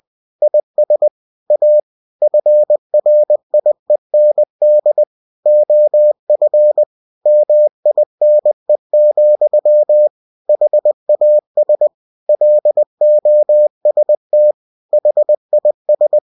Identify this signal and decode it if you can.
Morse code, heard as a keyed tone, about 20 wpm, 600 Hz. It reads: IS A FRIEND OF MINE, HAS LOST HIS
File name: cw-books / winnie-the-pooh / winnie-20wpm-page-0029